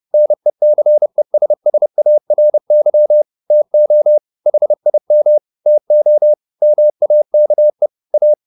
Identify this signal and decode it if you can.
Morse code, heard as a keyed tone, about 30 wpm, 600 Hz. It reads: NECESSARY TO HIM TO MAKE A
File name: the old man and the sea / the old man and the sea 1 30wpm